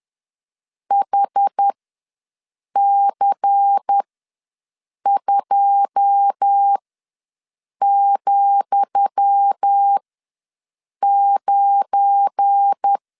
Morse code, heard as a keyed tone, about 10 wpm, spaced as for 6 wpm, 800 Hz. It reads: HC2,9
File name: LEZIONI / 10 SIGNS / 10D COMA ws 10wpm 4314